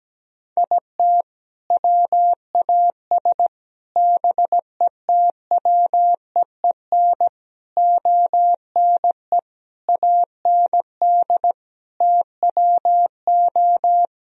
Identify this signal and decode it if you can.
Morse code, heard as a keyed tone, about 17 wpm, 700 Hz. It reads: IT WAS BETWEEN ONE AND TWO
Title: Sentences from Top 1000 Words - Review Part 1 - 17wpm